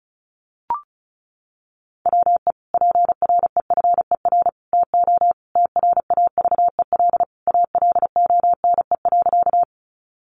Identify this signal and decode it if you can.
Morse code, heard as a keyed tone, about 34 wpm, 700 Hz. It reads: WE PREFER TO TRAVEL ALONE.
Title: Sentences from Top 1100 Words 35wpm